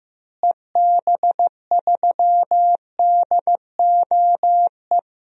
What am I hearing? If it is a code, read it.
Morse code, heard as a keyed tone, about 15 wpm, 700 Hz. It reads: EB3DOE
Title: DXCC-100 Call Signs - 6 Characters - Repeated 3x - 15wpm